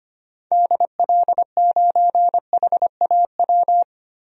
Morse code, heard as a keyed tone, about 25 wpm, 700 Hz. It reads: DL9HAW
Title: DXCC-100 Call Signs - 6 Characters - Repeated 3x - 25wpm